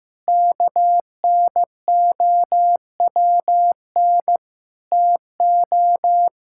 Morse code, heard as a keyed tone, about 15 wpm, 700 Hz. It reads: KNOWN TO
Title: Sentences from Top 1000 Words - Review Part 5 - 15wpm